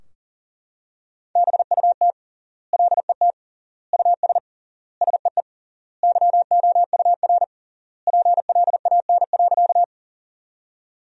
Morse code, heard as a keyed tone, about 40 wpm, 700 Hz. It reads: BUT LET US SEE YOUR PLAN.
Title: 3x Word-Spacing - Sentences from Top 500 Words - 40wpm